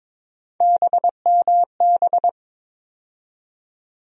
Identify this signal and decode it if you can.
Morse code, heard as a keyed tone, about 22 wpm, 700 Hz. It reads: BMB
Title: USA Call Sign Postfixes - 3 Letters 22wpm